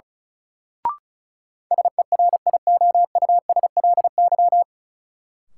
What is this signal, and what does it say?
Morse code, heard as a keyed tone, about 34 wpm, 700 Hz. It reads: SERIOUSLY
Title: Sentences from Top 1200 Words - New Word Drill - 35wpm